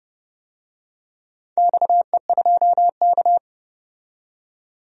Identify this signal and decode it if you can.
Morse code, heard as a keyed tone, about 30 wpm, 700 Hz. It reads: XE2K
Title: DXCC-100 Call Signs - 4 Characters - 30wpm